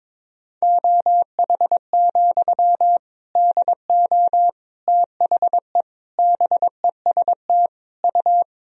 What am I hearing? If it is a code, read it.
Morse code, heard as a keyed tone, about 22 wpm, 700 Hz. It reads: OH, DO THE BEST U
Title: Sentences from Top 500 Words and Abbreviations 22wpm